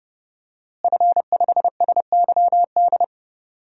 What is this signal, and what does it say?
Morse code, heard as a keyed tone, about 31 wpm, 700 Hz. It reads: F5SYD
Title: DXCC-100 Call Signs - 5 Characters - Repeated 2x - 30wpm